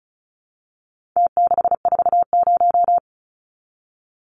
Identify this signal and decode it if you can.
Morse code, heard as a keyed tone, about 34 wpm, 700 Hz. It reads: T640
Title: Numbers - Contest Style - 4-Digit 35wpm